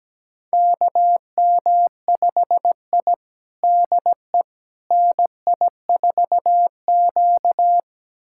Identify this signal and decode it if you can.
Morse code, heard as a keyed tone, about 17 wpm, 700 Hz. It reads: KM5I DE NI4Q
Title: USA Callsign DE Callsign - 4 Characters - 17wpm